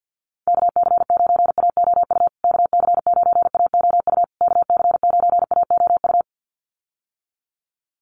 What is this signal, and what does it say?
Morse code, heard as a keyed tone, about 50 wpm, 700 Hz. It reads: KC9AOU KC9AOU KC9AOU
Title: USA Call Signs Repeated 3x 50wpm